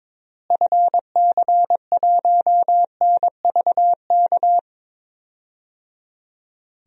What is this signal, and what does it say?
Morse code, heard as a keyed tone, about 21 wpm, 700 Hz. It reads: FC1NVK
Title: DXCC-100 Call Signs - 6 Characters - Repeated 3x - 22wpm